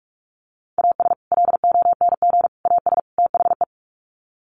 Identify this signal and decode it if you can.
Morse code, heard as a keyed tone, about 44 wpm, 700 Hz. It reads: AS LONG AS THE
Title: Sets of 4 Words 45wpm